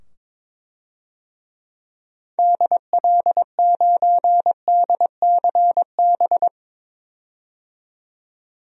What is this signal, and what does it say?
Morse code, heard as a keyed tone, about 22 wpm, 700 Hz. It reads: DL9DCB
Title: DXCC-100 Call Signs - 6 Characters - 22wpm